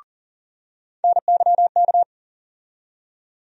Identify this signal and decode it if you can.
Morse code, heard as a keyed tone, about 40 wpm, 700 Hz. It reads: NYK
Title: USA Call Sign Postfixes - 3 Letters 40wpm